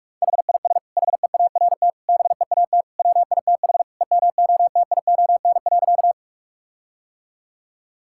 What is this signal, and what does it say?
Morse code, heard as a keyed tone, about 46 wpm, 700 Hz. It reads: HIS HEART BEAT WITH EMOTION.
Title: Sentences from Top 1600 Words 45wpm